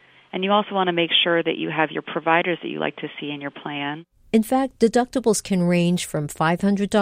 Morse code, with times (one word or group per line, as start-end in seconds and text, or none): none